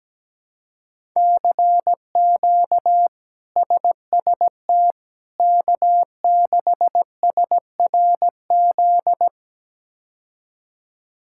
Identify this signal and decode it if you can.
Morse code, heard as a keyed tone, about 17 wpm, 700 Hz. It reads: CQ SST K6SRZ
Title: K1USN SST QSO Practice - 17wpm